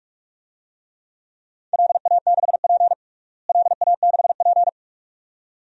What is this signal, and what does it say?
Morse code, heard as a keyed tone, about 45 wpm, 700 Hz. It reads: LA6P LA6P
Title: DXCC-100 Call Signs - 4 Characters - Repeated 2x - 45wpm